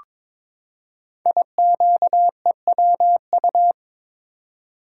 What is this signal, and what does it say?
Morse code, heard as a keyed tone, about 22 wpm, 700 Hz. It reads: IQEWU
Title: Code Groups - 5 Letters - 22wpm